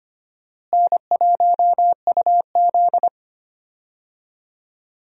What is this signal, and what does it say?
Morse code, heard as a keyed tone, about 25 wpm, 700 Hz. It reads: N1UZ
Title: USA Call Signs - 4 Characters - Repeated 2x - 25wpm